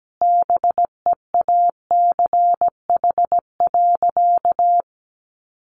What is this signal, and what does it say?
Morse code, heard as a keyed tone, about 17 wpm, 700 Hz. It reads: BEACH.
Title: Sentences from Top 1500 Words 17wpm